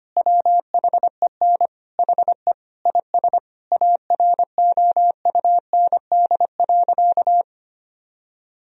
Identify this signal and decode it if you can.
Morse code, heard as a keyed tone, about 25 wpm, 700 Hz. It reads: WHEN HE IS AROUND.
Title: Sentences from Top 1000 Words - Review Part 4 - 25wpm